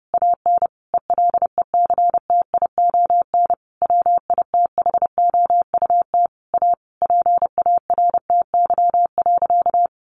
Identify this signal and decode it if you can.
Morse code, heard as a keyed tone, about 30 wpm, 700 Hz. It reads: AN ELECTION WITHOUT A PARTY.
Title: Sentences from Top 1500 Words 30wpm